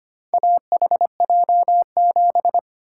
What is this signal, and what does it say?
Morse code, heard as a keyed tone, about 25 wpm, 700 Hz. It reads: AHJ7
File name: practice / 25wpm / Four Letter-Number Combinations - 25wpm